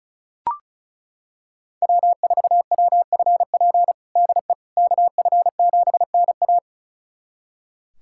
Morse code, heard as a keyed tone, about 36 wpm, 700 Hz. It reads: W4WFP DE KF7NA
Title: USA Callsign DE Callsign - 5 Characters - 35wpm